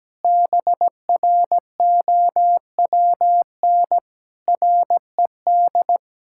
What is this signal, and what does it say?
Morse code, heard as a keyed tone, about 17 wpm, 700 Hz. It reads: BROWN RED
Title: Pangram Challenge v2 - 17wpm